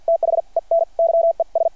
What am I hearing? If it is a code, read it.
Morse code, heard as a keyed tone, about 53 wpm, spaced as for 42 wpm, 660 Hz. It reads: THEN YES
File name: archived / CRSnet